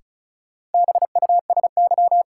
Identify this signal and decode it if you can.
Morse code, heard as a keyed tone, about 36 wpm, 700 Hz. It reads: BUSY
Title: Sentences from Top 1500 Words - New Word Drill Alt - 35wpm